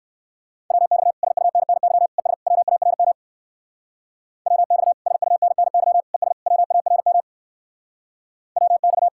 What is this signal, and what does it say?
Morse code, heard as a keyed tone, about 55 wpm, 700 Hz. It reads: WX SUNNY ES WARM WX SUNNY ES WARM WX
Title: Speed-Racing Common QSO Phrases - 45wpm